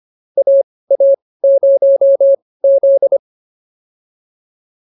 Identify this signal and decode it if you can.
Morse code, heard as a keyed tone, about 25 wpm, spaced as for 18 wpm, 550 Hz. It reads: AA0Z
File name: POTA201 18